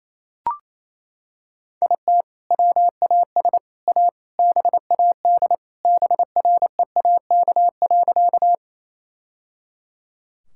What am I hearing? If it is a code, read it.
Morse code, heard as a keyed tone, about 28 wpm, 700 Hz. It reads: IT WAS A BAD BREAK.
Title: Sentences from Top 1000 Words - Review Part 6 - 28wpm